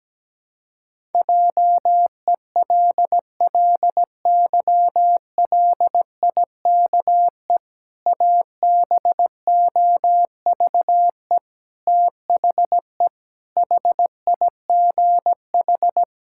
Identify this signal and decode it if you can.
Morse code, heard as a keyed tone, about 17 wpm, 700 Hz. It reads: JELLYLIKE ABOVE THE HIGH
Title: Pangram Challenge v2 - 17wpm